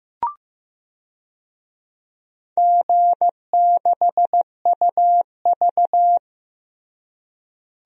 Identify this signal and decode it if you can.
Morse code, heard as a keyed tone, about 15 wpm, 700 Hz. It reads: G6UV
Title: DXCC-100 Call Signs - 4 Characters - Repeated 3x - 15wpm